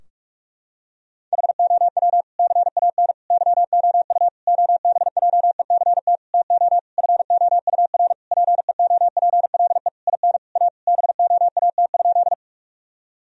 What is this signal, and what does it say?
Morse code, heard as a keyed tone, about 46 wpm, 700 Hz. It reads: HOW CAN YOU OBJECT TO FOUR PEOPLE IN A BOAT?
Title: Sentences from Top 1000 Words - Review Part 3 - 45wpm